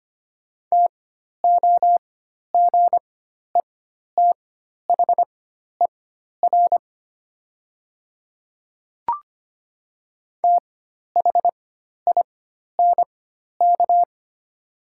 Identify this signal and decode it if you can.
Morse code, heard as a keyed tone, about 25 wpm, spaced as for 12 wpm, 700 Hz. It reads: TOGETHER THINK
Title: 4x Character Spacing - Top 300 Words - 25wpm